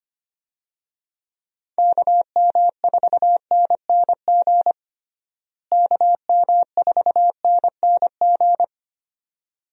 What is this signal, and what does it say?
Morse code, heard as a keyed tone, about 25 wpm, 700 Hz. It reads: KM4NNG KM4NNG
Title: USA Call Signs - 6 Characters - Repeated 2x - 25wpm